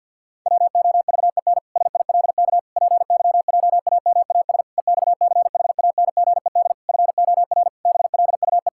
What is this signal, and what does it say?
Morse code, heard as a keyed tone, about 50 wpm, 700 Hz. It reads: WOVEN SILK PYJAMAS EXCHANGED FOR BLUE
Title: Pangram Challenge v2 - 50wpm